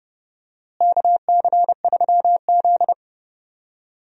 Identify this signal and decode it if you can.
Morse code, heard as a keyed tone, about 30 wpm, 700 Hz. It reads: KC3Z
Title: USA Call Signs - 4 Characters - Repeated 3x - 30wpm